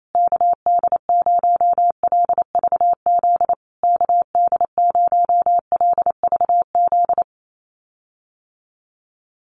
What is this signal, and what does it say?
Morse code, heard as a keyed tone, about 28 wpm, 690 Hz. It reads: KD0LVZ KD0LVZ
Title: USA Call Signs Repeated 3x 28wpm